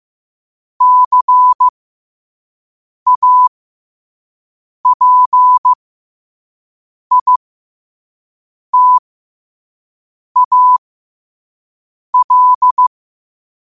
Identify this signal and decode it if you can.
Morse code, heard as a keyed tone, about 15 wpm, spaced as for 5 wpm, 1000 Hz. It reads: CAPITAL
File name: qotdfiles / 5wpm